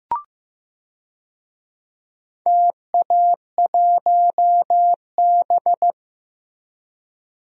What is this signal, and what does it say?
Morse code, heard as a keyed tone, about 16 wpm, 700 Hz. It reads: TA1B
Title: DXCC-100 Call Signs - 4 Characters - Repeated 3x - 15wpm